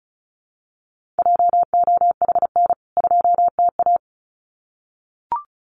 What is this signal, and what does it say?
Morse code, heard as a keyed tone, about 34 wpm, 700 Hz. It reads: JOHN 2TA
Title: CWT Exchange 35wpm